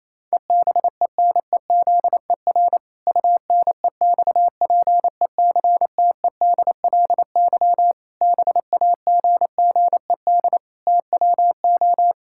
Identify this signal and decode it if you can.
Morse code, heard as a keyed tone, about 27 wpm, 700 Hz. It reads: EBENEZER UNEXPECTEDLY BAGGED TWO
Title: Pangram Challenge v2 - 28wpm